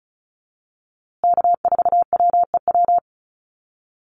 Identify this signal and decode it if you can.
Morse code, heard as a keyed tone, about 34 wpm, 700 Hz. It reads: K4WEW
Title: USA Call Signs Repeated 2x 35wpm